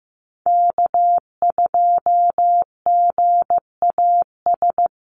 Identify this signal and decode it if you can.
Morse code, heard as a keyed tone, about 15 wpm, 700 Hz. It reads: K2GAS
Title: USA Call Signs Repeated 2x 15wpm